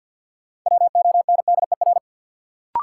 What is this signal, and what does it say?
Morse code, heard as a keyed tone, about 53 wpm, 700 Hz. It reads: WONDER
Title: Sentences from Top 500 Words - New Word Drill - 50wpm